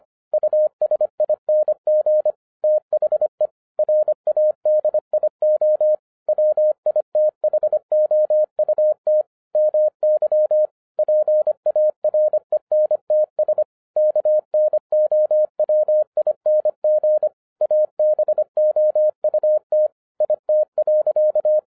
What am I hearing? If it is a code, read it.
Morse code, heard as a keyed tone, about 26 wpm, 610 Hz. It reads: USING THE RADIO WITHOUT MY PARENTS KNOWING ABOUT IT.